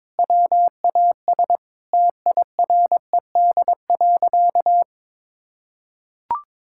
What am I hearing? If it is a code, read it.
Morse code, heard as a keyed tone, about 22 wpm, 700 Hz. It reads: WAS TIRED.
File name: practice / 22wpm / Sentences with 3-Word Phrases - 22wpm